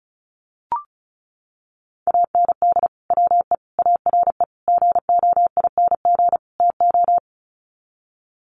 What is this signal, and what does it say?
Morse code, heard as a keyed tone, about 34 wpm, 700 Hz. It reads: AND WE ARE GOING TO
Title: Sets of 5 Words Encore2 35wpm